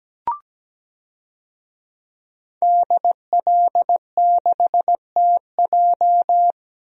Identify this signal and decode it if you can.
Morse code, heard as a keyed tone, about 17 wpm, 700 Hz. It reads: DL6TJ
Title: DXCC-100 Call Signs - 5 Characters - Repeated 3x - 17wpm